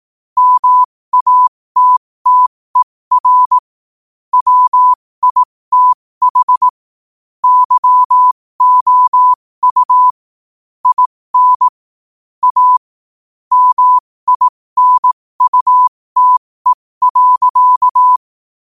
Morse code, heard as a keyed tone, about 18 wpm, spaced as for 15 wpm, 1000 Hz. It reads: MATTER WITH YOU IN A MINUTE.
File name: qotdfiles / 15wpm